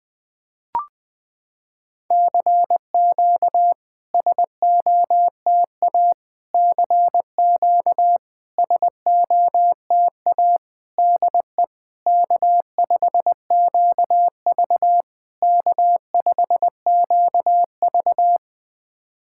Morse code, heard as a keyed tone, about 20 wpm, 700 Hz. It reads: CQ SOTA CQ SOTA DE K5QV K5QV
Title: SOTA QSO Practice 20wpm